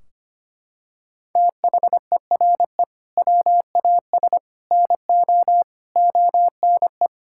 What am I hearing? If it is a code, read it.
Morse code, heard as a keyed tone, about 25 wpm, 700 Hz. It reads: THERE WAS NO ONE
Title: Sentences from Top 1600 Words 25wpm